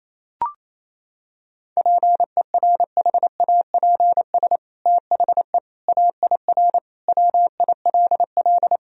Morse code, heard as a keyed tone, about 28 wpm, 700 Hz. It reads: PERHAPS THE AIR WILL
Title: Sentences from Top 1000 Words - Review Part 1 - 28wpm